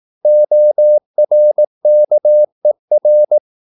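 Morse code, heard as a keyed tone, about 18 wpm, 600 Hz. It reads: ORKER
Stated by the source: Slow (QRS) Morse practice at 18 wpm, CW med Gnister.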